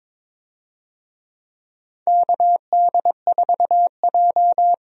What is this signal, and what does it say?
Morse code, heard as a keyed tone, about 21 wpm, 700 Hz. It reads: KD4J
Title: USA Call Signs - 4 Characters - Repeated 2x - 22wpm